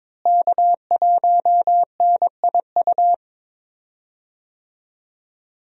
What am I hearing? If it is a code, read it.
Morse code, heard as a keyed tone, about 21 wpm, 700 Hz. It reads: K1NIU
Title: USA Call Signs - 5 Characters - Repeated 2x - 22wpm